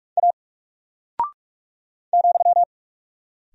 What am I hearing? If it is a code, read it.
Morse code, heard as a keyed tone, about 44 wpm, 700 Hz. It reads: A ,
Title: Single Character (Letter:Number:Punctuation:Prosign) 45wpm